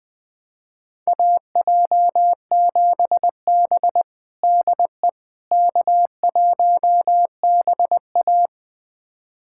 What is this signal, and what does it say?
Morse code, heard as a keyed tone, about 20 wpm, 700 Hz. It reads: AJ7B DE K1BA
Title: USA Callsign DE Callsign - 4 Characters - 20wpm